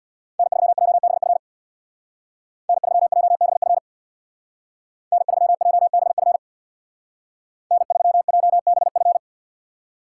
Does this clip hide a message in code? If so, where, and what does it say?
Yes: Morse code from 0.4 to 9.2 s, heard as a keyed tone, about 65 wpm, 700 Hz. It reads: N3JBF N3JBF N3JBF N3JBF